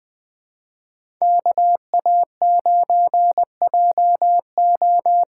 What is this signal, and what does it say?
Morse code, heard as a keyed tone, about 20 wpm, 700 Hz. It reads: KA9JO
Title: USA Call Signs - 6x Character Spacing - 20wpm